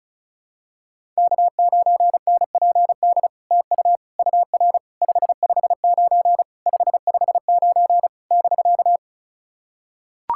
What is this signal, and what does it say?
Morse code, heard as a keyed tone, about 37 wpm, 700 Hz. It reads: K9NPD TU UR 559 559 <BK>
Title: SOTA QSO Practice 35wpm